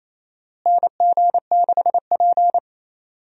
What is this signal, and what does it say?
Morse code, heard as a keyed tone, about 28 wpm, 700 Hz. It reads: NG6P
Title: USA Call Signs - 4 Characters - Repeated 3x - 28wpm